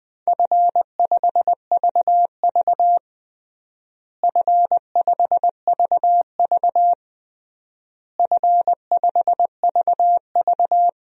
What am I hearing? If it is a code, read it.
Morse code, heard as a keyed tone, about 21 wpm, 700 Hz. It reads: F5VV F5VV F5VV
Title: DXCC-100 Call Signs - 4 Characters - Repeated 3x - 20wpm